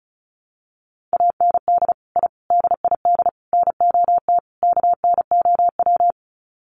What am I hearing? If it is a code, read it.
Morse code, heard as a keyed tone, about 34 wpm, 700 Hz. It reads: AND I DID NOT KNOW
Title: Sets of 5 Words Encore 35wpm